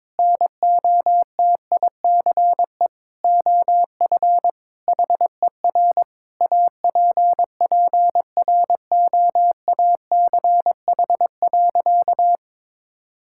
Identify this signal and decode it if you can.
Morse code, heard as a keyed tone, about 22 wpm, 700 Hz. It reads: NOTICE OF HER APPROACH.
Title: Sentences from Top 1000 Words - Review Part 6 - 22wpm